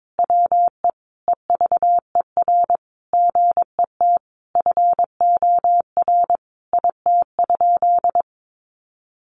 Morse code, heard as a keyed tone, about 22 wpm, 700 Hz. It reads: WE EVER GET FOR IT?
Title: Sentences from Top 400 Words 22wpm